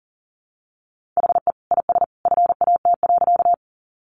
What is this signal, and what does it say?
Morse code, heard as a keyed tone, about 40 wpm, 700 Hz. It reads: HE IS FAT.
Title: Easy Phrases 40wpm